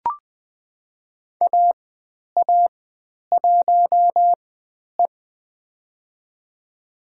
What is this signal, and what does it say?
Morse code, heard as a keyed tone, about 20 wpm, spaced as for 10 wpm, 700 Hz. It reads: AA1E
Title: 102 Lesson 17g - Letter u - Call Signs 20wpm